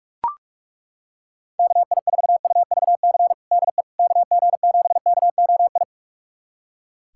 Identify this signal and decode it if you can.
Morse code, heard as a keyed tone, about 41 wpm, 700 Hz. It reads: KI4UVC DE KG7KOI
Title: USA Callsign DE Callsign - 6 Characters - 45wpm